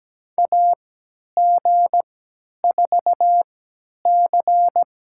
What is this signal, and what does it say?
Morse code, heard as a keyed tone, about 17 wpm, spaced as for 10 wpm, 700 Hz. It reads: AG4C
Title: USA Call Signs - 4 Characters - 3x Character Spacing - 17wpm